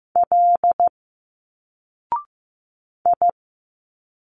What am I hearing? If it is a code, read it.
Morse code, heard as a keyed tone, about 15 wpm, 700 Hz. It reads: L I
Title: Single Letters 15wpm